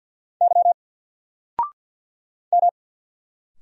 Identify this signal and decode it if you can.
Morse code, heard as a keyed tone, about 50 wpm, 700 Hz. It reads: Y M
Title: Single Letter-Number Alt 50wpm